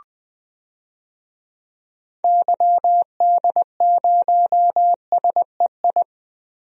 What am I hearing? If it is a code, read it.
Morse code, heard as a keyed tone, about 20 wpm, 700 Hz. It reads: YD0SEI
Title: DXCC-100 Call Signs - 6 Characters - Repeated 2x - 20wpm